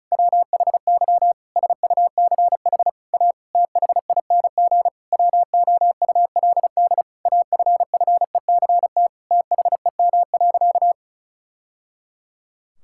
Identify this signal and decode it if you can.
Morse code, heard as a keyed tone, about 36 wpm, 700 Hz. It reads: WHY SUCH A THING WOULD AFFECT THEM.
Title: Sentences from Top 1000 Words - Review Part 6 - 35wpm